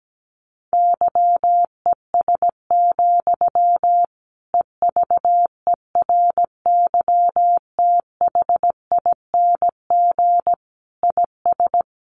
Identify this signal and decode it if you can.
Morse code, heard as a keyed tone, about 17 wpm, 700 Hz. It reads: YES, EVERYTHING IS
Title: Sentences from Top 1100 Words 17wpm